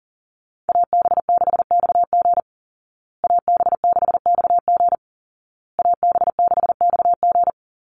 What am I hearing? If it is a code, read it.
Morse code, heard as a keyed tone, about 40 wpm, 700 Hz. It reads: AB6XG AB6XG AB6XG
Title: Call Signs Repeated 2x - 40wpm